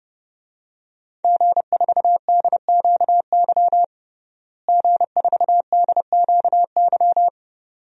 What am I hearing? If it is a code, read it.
Morse code, heard as a keyed tone, about 30 wpm, 700 Hz. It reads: G4DQY G4DQY
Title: DXCC-100 Call Signs - 5 Characters - Repeated 2x - 30wpm